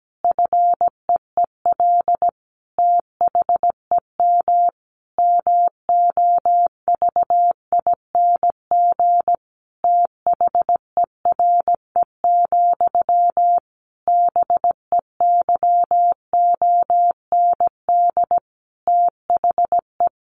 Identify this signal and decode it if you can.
Morse code, heard as a keyed tone, about 17 wpm, 700 Hz. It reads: FEEL THEM MOVING THERE, BEYOND THE
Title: A Sound Of Thunder 17wpm